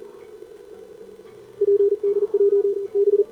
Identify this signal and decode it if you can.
Morse code, heard as a keyed tone, about 42 wpm, 400 Hz, with radio noise. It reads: PDJ B
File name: CQWWCW2025 / 20M SCAN CN3A